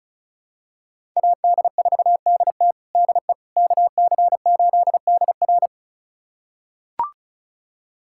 Morse code, heard as a keyed tone, about 36 wpm, 700 Hz. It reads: AD4DT DE KC8DR
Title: USA Callsign DE Callsign - 5 Characters - 35wpm